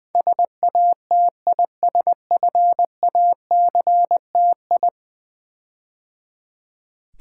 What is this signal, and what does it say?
Morse code, heard as a keyed tone, about 20 wpm, 700 Hz. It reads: SATISFACTI
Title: Build A Word - 5 Letters At A Time - 12-Letter Words - 20wpm